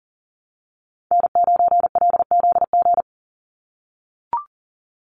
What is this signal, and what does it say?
Morse code, heard as a keyed tone, about 40 wpm, 700 Hz. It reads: N9LZG
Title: USA Call Signs 40wpm